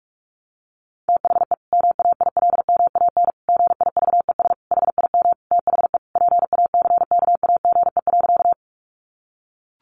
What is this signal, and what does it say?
Morse code, heard as a keyed tone, about 44 wpm, 700 Hz. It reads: THE MAILMAN GIVES HIM THE PACKAGE.
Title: Additional Easy Phrases 45wpm